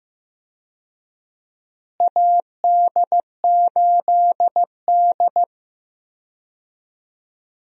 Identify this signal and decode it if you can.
Morse code, heard as a keyed tone, about 15 wpm, 700 Hz. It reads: AD8D
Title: USA Call Signs - 4 Characters - Repeated 2x - 15wpm